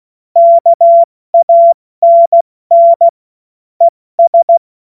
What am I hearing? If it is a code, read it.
Morse code, heard as a keyed tone, about 16 wpm, spaced as for 14 wpm, 680 Hz. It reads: KANN ES
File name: Morsekurs/Witze/Witz 12 Dezember 2023 70 F80